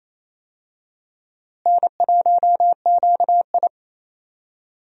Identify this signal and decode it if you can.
Morse code, heard as a keyed tone, about 28 wpm, 700 Hz. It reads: N1QI
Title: USA Call Signs - 4 Characters - Repeated 3x - 28wpm